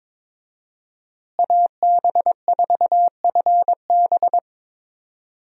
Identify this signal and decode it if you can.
Morse code, heard as a keyed tone, about 22 wpm, 700 Hz. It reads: AB4FB
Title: USA Call Signs - 5 Characters - Repeated 3x - 22wpm